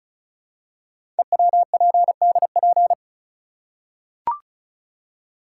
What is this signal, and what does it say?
Morse code, heard as a keyed tone, about 36 wpm, 700 Hz. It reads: EWPDP
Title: Code Groups - 5 Letters - 35wpm